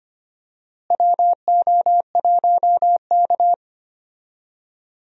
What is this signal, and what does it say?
Morse code, heard as a keyed tone, about 25 wpm, 700 Hz. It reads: WO1K
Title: USA Call Signs - 4 Characters - Repeated 2x - 25wpm